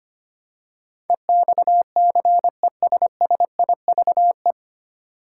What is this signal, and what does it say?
Morse code, heard as a keyed tone, about 25 wpm, 700 Hz. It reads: EXCESSIVE
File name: Nine Letter Words - 25wpm